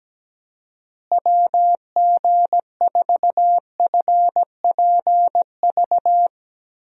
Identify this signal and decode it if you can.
Morse code, heard as a keyed tone, about 17 wpm, 700 Hz. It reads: WG4FPV
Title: USA Call Signs - 6 Characters - Repeated 2x - 17wpm